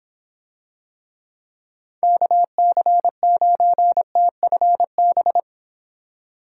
Morse code, heard as a keyed tone, about 26 wpm, 700 Hz. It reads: KC9TFB